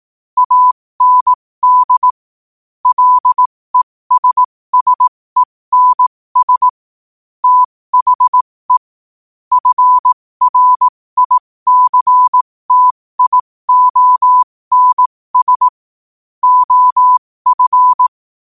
Morse code, heard as a keyed tone, about 18 wpm, spaced as for 15 wpm, 1000 Hz. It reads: AND LESSENS THE FRICTIONS OF
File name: 15wpm